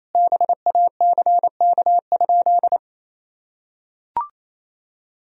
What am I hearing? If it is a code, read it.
Morse code, heard as a keyed tone, about 27 wpm, 700 Hz. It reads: BACK?
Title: Sentences from Top 1000 Words - Review Part 2 - 28wpm